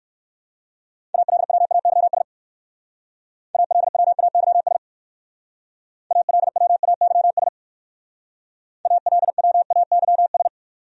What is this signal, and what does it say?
Morse code, heard as a keyed tone, about 54 wpm, 700 Hz. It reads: ALWAYS ALWAYS ALWAYS ALWAYS
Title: Speed-Racing Top 300 Words 45wpm